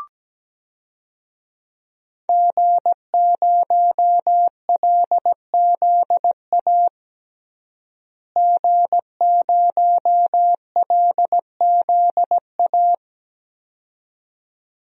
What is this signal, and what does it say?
Morse code, heard as a keyed tone, about 17 wpm, 700 Hz. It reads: G0LZA G0LZA
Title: DXCC-100 Call Signs - 5 Characters - Repeated 2x - 17wpm